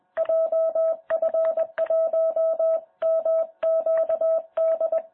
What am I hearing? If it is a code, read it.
Morse code, heard as a keyed tone, about 21 wpm, 650 Hz. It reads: JF1MQD
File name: jh8cbh / 2021cw